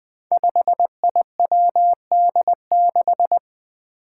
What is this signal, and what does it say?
Morse code, heard as a keyed tone, about 20 wpm, 700 Hz. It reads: 5IWD6
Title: Five Challenging Letter-Number Combinations 20wpm